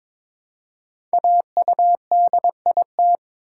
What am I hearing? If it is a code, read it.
Morse code, heard as a keyed tone, about 21 wpm, 700 Hz. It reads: AUDIT
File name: 5 Letter Words - First 3 Letters Given - 22wpm